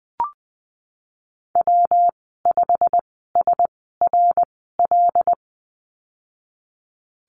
Morse code, heard as a keyed tone, about 20 wpm, spaced as for 15 wpm, 700 Hz. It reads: W5SRL
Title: 140 Lesson 23g - Letter f - Call Signs 20wpm